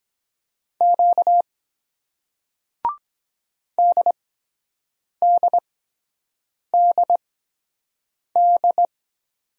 Morse code, heard as a keyed tone, about 23 wpm, spaced as for 7 wpm, 700 Hz. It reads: Q DDDD